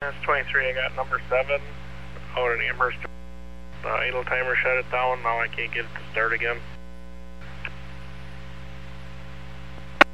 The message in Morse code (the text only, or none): none